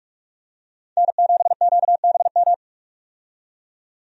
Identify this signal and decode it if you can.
Morse code, heard as a keyed tone, about 44 wpm, 700 Hz. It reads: N7QBM
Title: USA Call Signs - 5 Characters - 45wpm